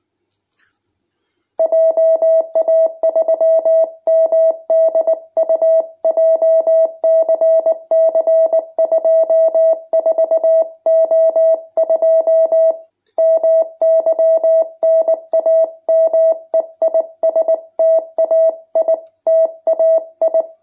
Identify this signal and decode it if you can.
Morse code, heard as a keyed tone, about 19 wpm, 650 Hz. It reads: JA3MDU JCC 24O2 MY NAME IS TAI TAI
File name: jh8cbh / 2021cw